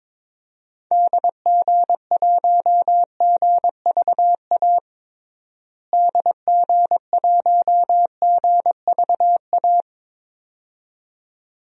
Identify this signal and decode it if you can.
Morse code, heard as a keyed tone, about 22 wpm, 700 Hz. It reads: DG1GVA DG1GVA
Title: DXCC-100 Call Signs - 6 Characters - Repeated 2x - 22wpm